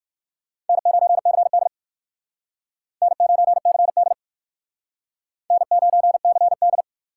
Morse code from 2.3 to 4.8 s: N9CD

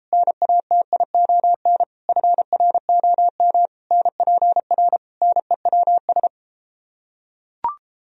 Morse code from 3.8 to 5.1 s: NPR

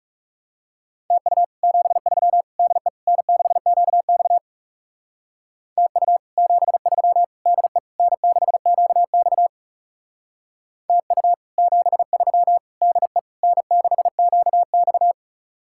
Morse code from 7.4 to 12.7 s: DE N6QX TU 73

